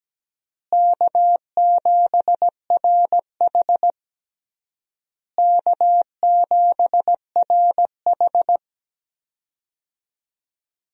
Morse code, heard as a keyed tone, about 17 wpm, 700 Hz. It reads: K7RH K7RH
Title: USA Call Signs - 4 Characters - Repeated 3x - 17wpm